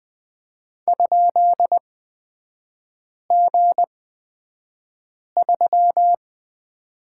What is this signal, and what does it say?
Morse code, heard as a keyed tone, about 20 wpm, spaced as for 5 wpm, 690 Hz. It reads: ?G3
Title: 236 Lesson 38e - Letter q - Three Characters 20wpm